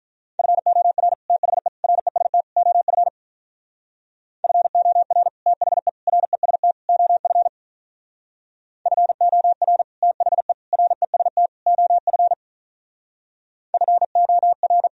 Morse code, heard as a keyed tone, about 45 wpm, 700 Hz. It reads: FOR THE REST OF FOR THE REST OF FOR THE REST OF FOR